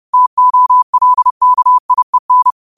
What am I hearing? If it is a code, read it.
Morse code, heard as a keyed tone, about 30 wpm, 1000 Hz. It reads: TOLKIEN
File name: qotdfiles / 30wpm